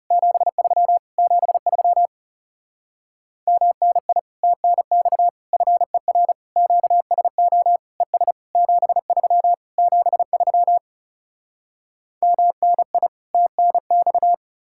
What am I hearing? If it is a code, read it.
Morse code, heard as a keyed tone, about 37 wpm, 700 Hz. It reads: 73 73 MNI TNX FER QSO ES 73 73 MNI TNX
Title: Speed-Racing Common QSO Phrases Encore - 30wpm